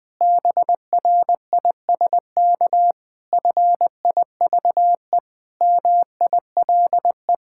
Morse code, heard as a keyed tone, about 20 wpm, 700 Hz. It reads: BRISK FIVE MILE
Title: Difficult Sentences with Hints - 20wpm